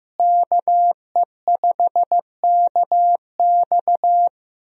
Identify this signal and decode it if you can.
Morse code, heard as a keyed tone, about 15 wpm, 700 Hz. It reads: KE5KX
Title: USA Call Signs - 5 Characters - Repeated 3x - 15wpm